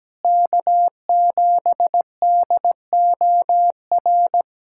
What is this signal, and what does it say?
Morse code, heard as a keyed tone, about 17 wpm, 700 Hz. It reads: K7DOR
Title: USA Callsign DE Callsign - 5 Characters - 17wpm